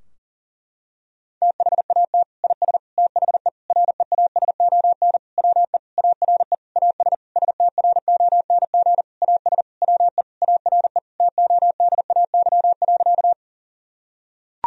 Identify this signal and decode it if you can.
Morse code, heard as a keyed tone, about 40 wpm, 700 Hz. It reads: THAT IS THE REASON WE ARE AS STRONG AS WE ARE TODAY.
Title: Sentences from Top 1000 Words - Review Part 4 - 40wpm